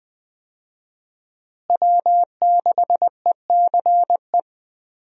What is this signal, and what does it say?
Morse code, heard as a keyed tone, about 20 wpm, 700 Hz. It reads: W6ECE
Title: USA Call Signs - 5 Characters - Repeated 2x - 20wpm